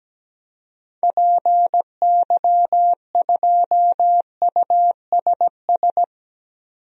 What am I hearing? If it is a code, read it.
Morse code, heard as a keyed tone, about 17 wpm, 700 Hz. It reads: PY2USS
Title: DXCC-100 Call Signs - 6 Characters - 17wpm